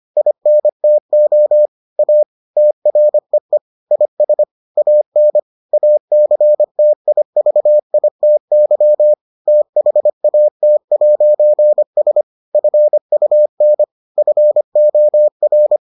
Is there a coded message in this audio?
Yes: Morse code from 0.2 to 15.8 s, heard as a keyed tone, about 25 wpm, 600 Hz. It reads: INTO A TREE IS AN ACTIVITY THAT'S FUN FOR